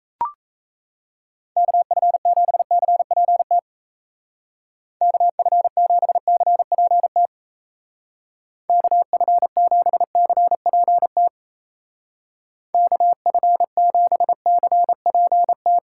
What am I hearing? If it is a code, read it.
Morse code, heard as a keyed tone, about 33 wpm, 700 Hz. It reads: KF7CPT KF7CPT KF7CPT KF7CPT
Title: Speed Racing - USA Call Signs 28wpm